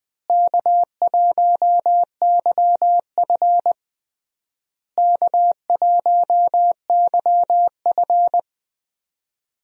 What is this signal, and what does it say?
Morse code, heard as a keyed tone, about 20 wpm, 700 Hz. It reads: K1YF K1YF
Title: USA Call Signs - 4 Characters - Repeated 3x - 20wpm